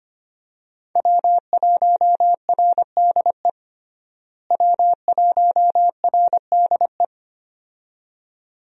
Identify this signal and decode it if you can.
Morse code, heard as a keyed tone, about 25 wpm, 690 Hz. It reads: W1RDE W1RDE
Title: USA Call Signs - 5 Characters - Repeated 2x - 25wpm